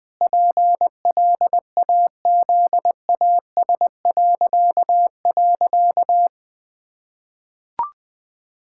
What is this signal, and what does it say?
Morse code, heard as a keyed tone, about 20 wpm, 700 Hz. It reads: PLAZAS..
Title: Pangram Challenge v2 - 20wpm